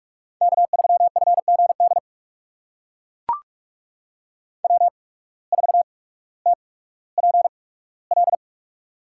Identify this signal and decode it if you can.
Morse code, heard as a keyed tone, about 44 wpm, spaced as for 19 wpm, 700 Hz. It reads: K3FGD W4TPL
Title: USA Call Signs - 5 Characters - 8x Character Spacing - 45wpm